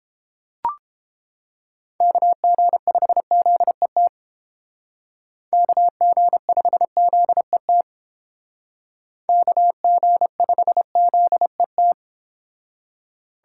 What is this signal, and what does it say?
Morse code, heard as a keyed tone, about 30 wpm, 700 Hz. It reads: KG5ZET KG5ZET KG5ZET